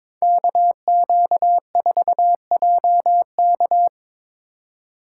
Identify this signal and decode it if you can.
Morse code, heard as a keyed tone, about 24 wpm, 700 Hz. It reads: KQ4JK
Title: Speed Racing - USA 2x2 Call Signs 22wpm